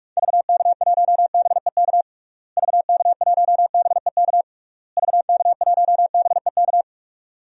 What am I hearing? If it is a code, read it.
Morse code, heard as a keyed tone, about 42 wpm, 700 Hz. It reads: VK1BEK VK1BEK VK1BEK
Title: DXCC-100 Call Signs - 6 Characters - Repeated 3x - 45wpm